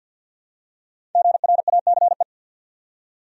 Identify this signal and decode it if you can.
Morse code, heard as a keyed tone, about 51 wpm, 700 Hz. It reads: GRACE
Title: USA Common Names - 50wpm